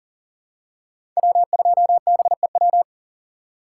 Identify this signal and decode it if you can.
Morse code, heard as a keyed tone, about 40 wpm, 700 Hz. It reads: W2BEW
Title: USA Call Signs - 5 Characters - 4x Character Spacing - 40wpm